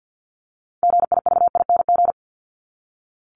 Morse code, heard as a keyed tone, about 48 wpm, 700 Hz. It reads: GIVING
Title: ING Suffix - Top 100 Words with Base Words 50wpm